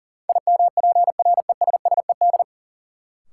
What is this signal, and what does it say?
Morse code, heard as a keyed tone, about 40 wpm, 700 Hz. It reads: IMPRESSED
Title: Sentences from Top 1600 Words - New Word Drill Alt - 40wpm